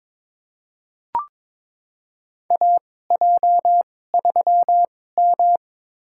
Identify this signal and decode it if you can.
Morse code, heard as a keyed tone, about 21 wpm, spaced as for 16 wpm, 700 Hz. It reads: AJ3M
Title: USA Call Signs - 4 Characters - 2x Character Spacing - 22wpm